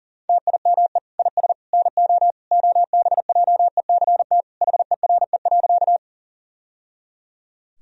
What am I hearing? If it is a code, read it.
Morse code, heard as a keyed tone, about 40 wpm, 700 Hz. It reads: TIME IS NO OBJECT HERE.
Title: Sentences from Top 1000 Words - Review Part 3 - 40wpm